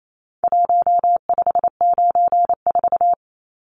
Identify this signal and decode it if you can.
Morse code, heard as a keyed tone, about 28 wpm, 700 Hz. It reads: 1594
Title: Numbers - Contest Style - 4-Digit 28wpm